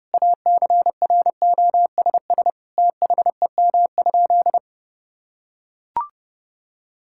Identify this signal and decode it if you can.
Morse code, heard as a keyed tone, about 30 wpm, 700 Hz. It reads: ACROSS THEM?
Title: Sentences from Top 1000 Words - Review Part 5 - 30wpm